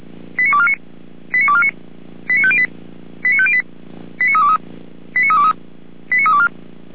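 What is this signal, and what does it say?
Five-tone selective call (ZVEI), heard as quick runs of five tones, tones of about 70 ms. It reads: 89249 89249 89508 89508 89323 89323 89324
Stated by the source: Five-tone ZVEI selective call 89570, received by SDR.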